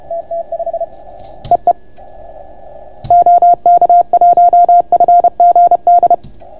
Morse code, heard as a keyed tone, about 31 wpm, 690 Hz. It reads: TT5 EE OK1FGD